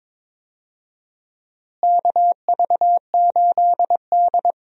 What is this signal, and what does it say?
Morse code, heard as a keyed tone, about 22 wpm, 700 Hz. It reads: KV8D
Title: USA Call Signs - 4 Characters - Repeated 2x - 22wpm